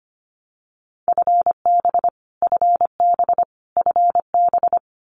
Morse code, heard as a keyed tone, about 25 wpm, 700 Hz. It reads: FB FB FB
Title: Abbreviations Repeated 3x - 25wpm